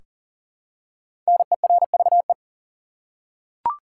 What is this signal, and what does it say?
Morse code, heard as a keyed tone, about 37 wpm, 700 Hz. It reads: NERVE
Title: Five Letter Words - 40wpm